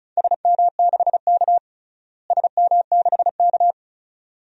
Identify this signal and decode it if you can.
Morse code, heard as a keyed tone, about 34 wpm, 700 Hz. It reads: SM6K SM6K
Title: DXCC-100 Call Signs - 4 Characters - Repeated 2x - 35wpm